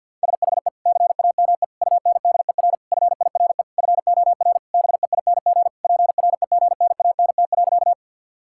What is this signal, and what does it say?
Morse code, heard as a keyed tone, about 50 wpm, 700 Hz. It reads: SHE CAME UNDER FIRE FOR BEING PREGNANT.
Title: Sentences from Top 1000 Words - Review Part 6 - 50wpm